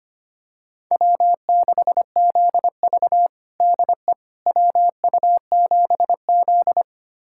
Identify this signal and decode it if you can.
Morse code, heard as a keyed tone, about 25 wpm, 700 Hz. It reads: W6ZV DE WU7Z
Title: USA Callsign DE Callsign - 4 Characters - 25wpm